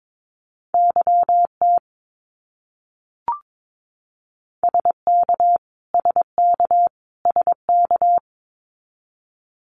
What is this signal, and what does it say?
Morse code, heard as a keyed tone, about 22 wpm, 700 Hz. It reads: YT SK SK SK
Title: Canadian Provinces and Territories - Repeated 3x - 22wpm